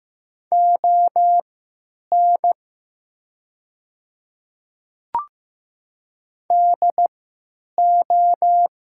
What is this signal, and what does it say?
Morse code, heard as a keyed tone, about 15 wpm, spaced as for 9 wpm, 700 Hz. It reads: ON DO
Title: 3x Character Spacing - Top 200 Words - 15wpm